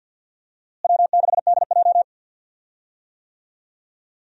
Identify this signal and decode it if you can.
Morse code, heard as a keyed tone, about 51 wpm, 700 Hz. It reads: W6DJ
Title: USA Call Signs - 4 Characters - Repeated 3x - 50wpm